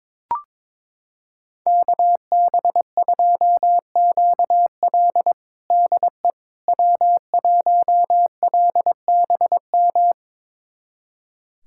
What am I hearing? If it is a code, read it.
Morse code, heard as a keyed tone, about 22 wpm, 700 Hz. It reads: KB2QL DE W1LBM
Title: USA Callsign DE Callsign - 5 Characters - 22wpm